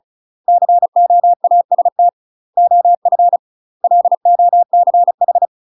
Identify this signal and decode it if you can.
Morse code, heard as a keyed tone, about 36 wpm, 700 Hz. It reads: COAST OF LOCH
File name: UK MASTER QSOs TRAINING / 35WPM / UK MASTER QSOs 0038